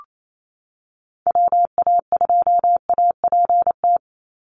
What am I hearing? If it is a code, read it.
Morse code, heard as a keyed tone, about 28 wpm, 700 Hz. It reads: WA2APT
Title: USA Call Signs 28wpm